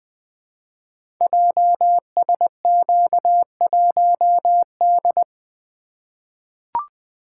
Morse code, heard as a keyed tone, about 20 wpm, 700 Hz. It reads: JSQ1D
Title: Code Groups - 5 Letter:Numbers - 20wpm